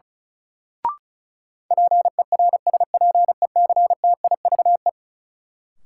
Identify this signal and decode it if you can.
Morse code, heard as a keyed tone, about 36 wpm, 700 Hz. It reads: PERSPECTIVE
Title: Sentences from Top 1200 Words - New Word Drill Alt - 35wpm